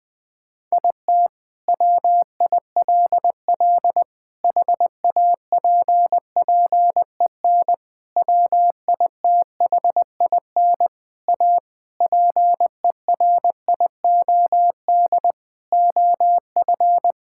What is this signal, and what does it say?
Morse code, heard as a keyed tone, about 20 wpm, 700 Hz. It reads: IT WILL HAPPEN WITHIN A PERIOD OF
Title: Sentences from Top 1000 Words - Review Part 5 - 20wpm